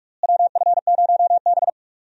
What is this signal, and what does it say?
Morse code, heard as a keyed tone, about 42 wpm, 700 Hz. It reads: WF0B
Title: USA Call Signs - 4 Characters - Repeated 3x - 45wpm